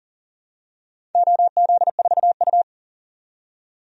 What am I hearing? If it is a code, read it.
Morse code, heard as a keyed tone, about 40 wpm, 700 Hz. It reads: OZ4U
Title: DXCC-100 Call Signs - 4 Characters - 40wpm